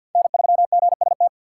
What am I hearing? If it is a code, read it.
Morse code, heard as a keyed tone, about 46 wpm, 700 Hz. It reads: N3GST
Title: USA Call Signs - 5 Characters - 4x Character Spacing - 50wpm